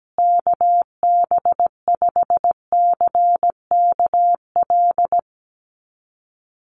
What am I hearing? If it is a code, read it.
Morse code, heard as a keyed tone, about 17 wpm, 700 Hz. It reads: KB5CKL
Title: USA Call Signs Repeated 3x 17wpm